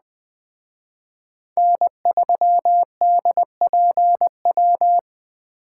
Morse code, heard as a keyed tone, about 19 wpm, 700 Hz. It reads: N3DPW